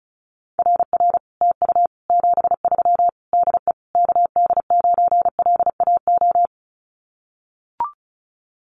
Morse code, heard as a keyed tone, about 34 wpm, 700 Hz. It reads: RR TU 73 DE KD9LAO
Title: Common QSO Phrases Encore 35wpm